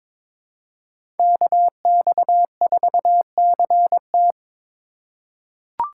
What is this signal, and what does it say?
Morse code, heard as a keyed tone, about 21 wpm, 700 Hz. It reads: KX4CT
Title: USA Call Signs - 5 Characters - 4x Character Spacing - 22wpm